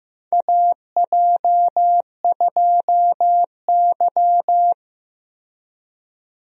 Morse code, heard as a keyed tone, about 15 wpm, 700 Hz. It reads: AJ2Y
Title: USA Call Signs - 4 Characters - Repeated 3x - 15wpm